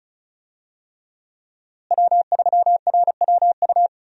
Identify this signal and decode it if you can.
Morse code, heard as a keyed tone, about 36 wpm, 700 Hz. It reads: W3RWU
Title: USA Call Signs - 5 Characters - Repeated 3x - 35wpm